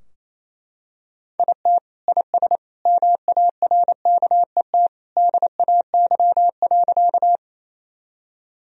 Morse code, heard as a keyed tone, about 28 wpm, 700 Hz. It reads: IT IS MARKET DAY.
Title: Sentences from Top 1000 Words - Review Part 3 - 28wpm